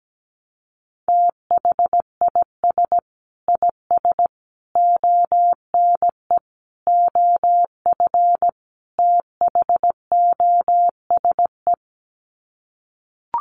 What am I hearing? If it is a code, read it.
Morse code, heard as a keyed tone, about 17 wpm, 700 Hz. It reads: THIS IS ONE OF THOSE
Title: Sets of 5 Words Encore2 17wpm